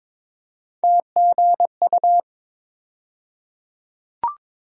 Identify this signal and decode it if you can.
Morse code, heard as a keyed tone, about 22 wpm, 700 Hz. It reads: TGU
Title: USA Call Sign Postfixes - 3 Letters 22wpm